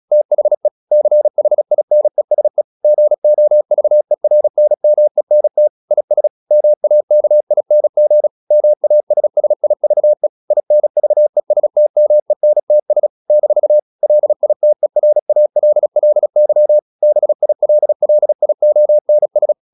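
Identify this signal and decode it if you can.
Morse code, heard as a keyed tone, about 36 wpm, 600 Hz. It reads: THE CHINESE GOVERNMENT IS MAKING MASSIVE INVESTMENTS - LITERALLY BILLIONS
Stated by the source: Morse practice at 36 wpm, text of Ken Follett's novel Never, from CW med Gnister.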